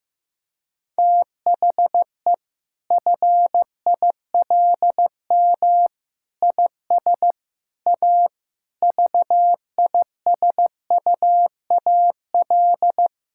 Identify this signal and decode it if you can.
Morse code, heard as a keyed tone, about 15 wpm, 700 Hz. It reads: THE FILM IS A VISUAL
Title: Sentences from Top 1000 Words - Review Part 6 - 15wpm